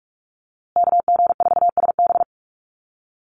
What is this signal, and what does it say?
Morse code, heard as a keyed tone, about 44 wpm, 700 Hz. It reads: KG4SB
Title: USA Call Signs Repeated 3x 45wpm